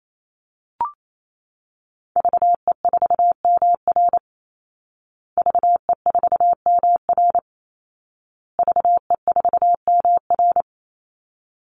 Morse code, heard as a keyed tone, about 28 wpm, 700 Hz. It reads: VE4MR VE4MR VE4MR
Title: Call Signs Repeated 2x - 28wpm